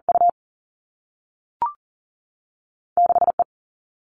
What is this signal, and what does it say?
Morse code, heard as a keyed tone, about 40 wpm, 700 Hz. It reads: U 6 E
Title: Two Challenging Letter-Number Combinations 40wpm